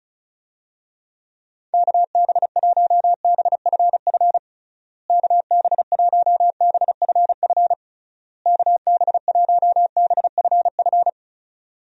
Morse code, heard as a keyed tone, about 34 wpm, 700 Hz. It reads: KB1BFF KB1BFF KB1BFF
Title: USA Call Signs - 6 Characters - Repeated 3x - 35wpm